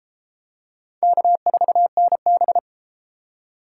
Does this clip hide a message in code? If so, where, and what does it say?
Yes: Morse code from 1.0 to 2.6 s, heard as a keyed tone, about 28 wpm, 700 Hz. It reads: K4NB